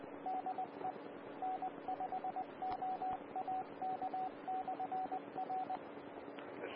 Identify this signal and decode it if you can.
Morse code, heard as a keyed tone, about 20 wpm, 750 Hz, with radio noise. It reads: DE N5OAK/R
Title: Austin, TX Amateur Repeaters